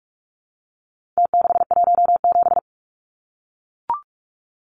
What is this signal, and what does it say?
Morse code, heard as a keyed tone, about 44 wpm, 700 Hz. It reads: T617
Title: Numbers - Contest Style - 4-Digit 45wpm